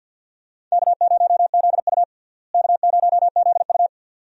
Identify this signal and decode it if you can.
Morse code, heard as a keyed tone, about 50 wpm, 700 Hz. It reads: K0ZU K0ZU
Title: USA Call Signs - 4 Characters - Repeated 2x - 50wpm